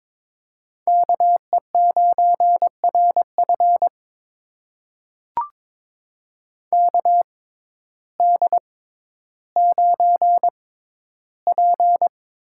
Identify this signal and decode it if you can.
Morse code, heard as a keyed tone, about 22 wpm, spaced as for 12 wpm, 700 Hz. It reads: KE9RF KD9P
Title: USA Call Signs - 5 Characters - 6x Character Spacing - 22wpm